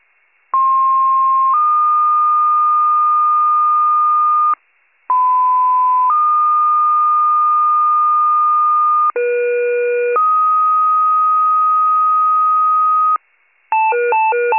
Two-tone page, heard as a one-second tone, then a three-second tone, about 1060 then 1220 Hz, about 1010 then 1220 Hz, about 480 then 1250 Hz. It is heard from 0.5 to 13.2 s.